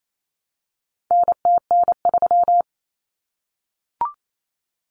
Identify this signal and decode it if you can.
Morse code, heard as a keyed tone, about 28 wpm, 700 Hz. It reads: NTN3
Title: Common Cut Numbers - 4-Digit 28wpm